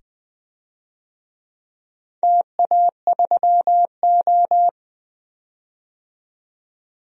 Morse code, heard as a keyed tone, about 20 wpm, 700 Hz. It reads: TA3O
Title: DXCC-100 Call Signs - 4 Characters - 20wpm